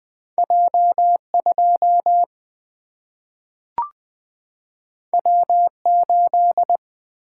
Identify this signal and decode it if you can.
Morse code, heard as a keyed tone, about 20 wpm, 700 Hz. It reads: J2 W8
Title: Single-Letter Single-Number 20wpm